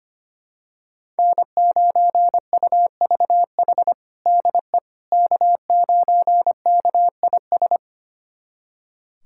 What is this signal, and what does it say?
Morse code, heard as a keyed tone, about 25 wpm, 700 Hz. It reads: N9UVH DE K9KIS
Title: USA Callsign DE Callsign - 5 Characters - 25wpm